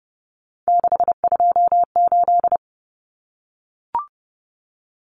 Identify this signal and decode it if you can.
Morse code, heard as a keyed tone, about 29 wpm, 700 Hz. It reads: 628
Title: Numbers - 3-Digit 30wpm